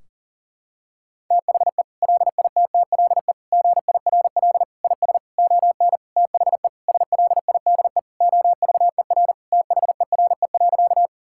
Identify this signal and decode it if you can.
Morse code, heard as a keyed tone, about 40 wpm, 700 Hz. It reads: THE LITTLE GIRL IS ON THE SLIDE OVER THERE.
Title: Sentences from Top 1600 Words 40wpm